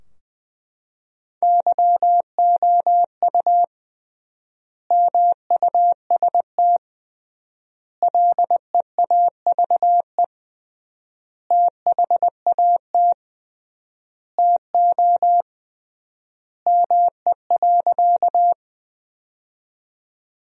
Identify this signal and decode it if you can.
Morse code, heard as a keyed tone, about 21 wpm, 700 Hz. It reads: YOU MUST LEAVE THAT TO ME.
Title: 3x Word-Spacing - Sentences from Top 400 Words - 20wpm